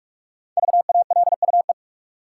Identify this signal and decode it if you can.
Morse code, heard as a keyed tone, about 46 wpm, 700 Hz. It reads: VALUE
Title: Sentences from Top 600 Words - New Word Drill - 45wpm